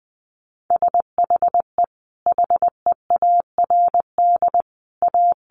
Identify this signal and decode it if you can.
Morse code, heard as a keyed tone, about 19 wpm, 700 Hz. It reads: SHE HEARD A
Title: Sentences from Top 500 Words 20wpm